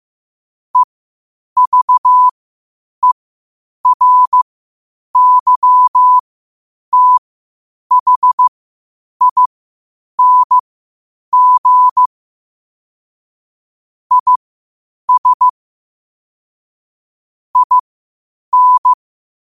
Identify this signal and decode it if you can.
Morse code, heard as a keyed tone, about 15 wpm, spaced as for 8 wpm, 1000 Hz. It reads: EVERYTHING IS IN
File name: qotdfiles / 8wpm